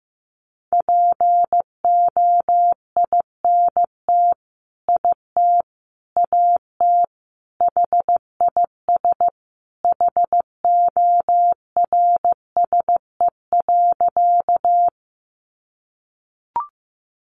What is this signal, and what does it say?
Morse code, heard as a keyed tone, about 15 wpm, 700 Hz. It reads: POINT IT AT HIS HORSE.
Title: Sentences from Top 300 Words 15wpm